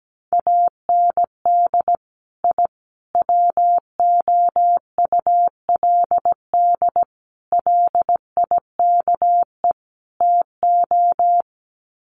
Morse code, heard as a keyed tone, about 17 wpm, 700 Hz. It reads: AND I WOULD LIKE TO
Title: Sets of 5 Words Encore 17wpm